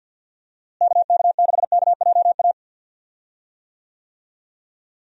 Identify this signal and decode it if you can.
Morse code, heard as a keyed tone, about 51 wpm, 700 Hz. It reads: KK6KJA
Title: USA Call Signs - 6 Characters - Repeated 3x - 50wpm